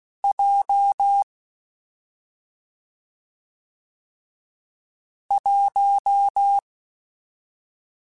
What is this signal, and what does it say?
Morse code, heard as a keyed tone, about 16 wpm, 800 Hz. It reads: J 1
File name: Difficult Combos / Difficult Combos-mp3s / AWJ1 @5